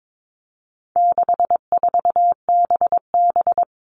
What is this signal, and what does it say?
Morse code, heard as a keyed tone, about 22 wpm, 700 Hz. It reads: 64BB
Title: All Cut Numbers - 4-Digit 22wpm